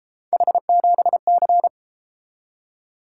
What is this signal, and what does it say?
Morse code, heard as a keyed tone, about 32 wpm, 700 Hz. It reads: H7C